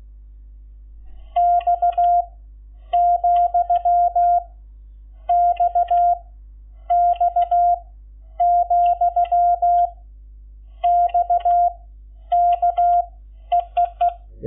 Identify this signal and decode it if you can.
Morse code, heard as a keyed tone, about 16 wpm, 700 Hz. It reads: X , X X , X K EEE
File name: Lecciones / CLASE Nº11